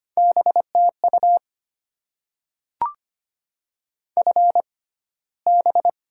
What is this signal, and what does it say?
Morse code, heard as a keyed tone, about 25 wpm, spaced as for 15 wpm, 690 Hz. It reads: BTU FB
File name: Top 20 QSO Elements - 6x Character Spacing - 25wpm